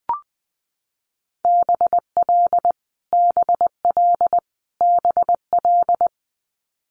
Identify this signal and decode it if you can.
Morse code, heard as a keyed tone, about 20 wpm, 700 Hz. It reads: BL BL BL
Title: Multi-letter Phonemes Repeated 3x 20wpm